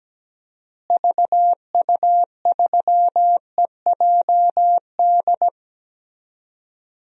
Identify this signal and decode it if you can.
Morse code, heard as a keyed tone, about 17 wpm, 700 Hz. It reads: VU3EJD
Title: DXCC-100 Call Signs - 6 Characters - Repeated 3x - 17wpm